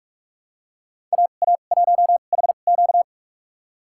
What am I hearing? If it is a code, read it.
Morse code, heard as a keyed tone, about 46 wpm, spaced as for 33 wpm, 700 Hz. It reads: AA1HQ
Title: USA Call Signs - 5 Characters - 2x Character Spacing - 45wpm